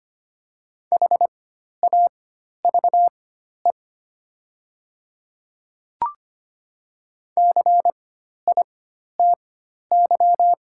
Morse code, heard as a keyed tone, about 25 wpm, spaced as for 12 wpm, 700 Hz. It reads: HAVE CITY
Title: 4x Character Spacing - Top 300 Words - 25wpm